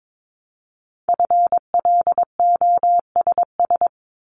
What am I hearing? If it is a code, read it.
Morse code, heard as a keyed tone, about 22 wpm, 700 Hz. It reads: FLOSS
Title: Multi-letter Phonemes 22wpm